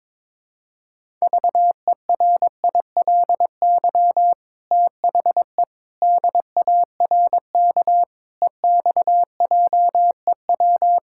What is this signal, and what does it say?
Morse code, heard as a keyed tone, about 22 wpm, 700 Hz. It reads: VERILY THE DARK EXJEW
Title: Pangram Challenge v2 - 22wpm